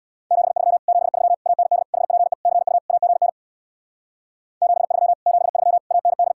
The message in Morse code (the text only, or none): 73 73 AND HPE CU AGN 73 73 AND